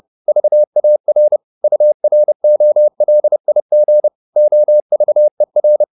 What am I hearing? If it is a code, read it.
Morse code, heard as a keyed tone, about 30 wpm, 600 Hz. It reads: VAR UROLIG OVER
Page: Ferie 30 wpm